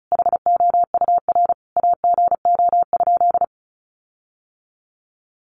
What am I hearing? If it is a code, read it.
Morse code, heard as a keyed tone, about 34 wpm, 700 Hz. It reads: HOUR AGO?
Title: Sentences from Top 1300 Words 35wpm